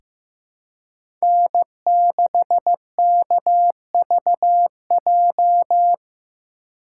Morse code, heard as a keyed tone, about 15 wpm, 700 Hz. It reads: N6KVJ
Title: USA Call Signs - 5 Characters - 4x Character Spacing - 15wpm